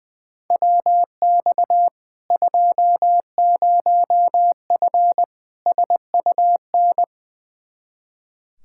Morse code, heard as a keyed tone, about 21 wpm, 700 Hz. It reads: WX 20F SUN
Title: Practice QSO - Exchange 2 - Info - 20wpm